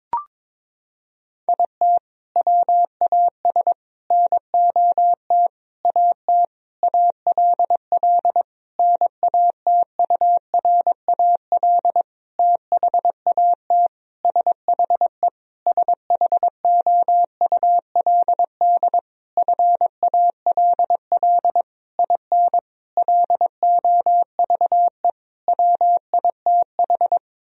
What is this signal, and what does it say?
Morse code, heard as a keyed tone, about 22 wpm, 700 Hz. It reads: IT WAS NOT AT ALL NATURAL THAT SHE SHOULD FALL IN LOVE WITH
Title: Sentences from Top 1000 Words - Review Part 4 - 22wpm